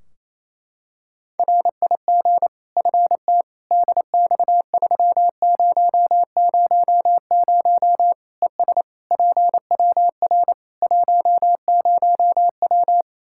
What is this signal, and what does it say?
Morse code, heard as a keyed tone, about 28 wpm, 700 Hz. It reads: RIG FT DX3000 ES PWR 10W
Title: Practice QSO - Exchange 2 - Info - 28wpm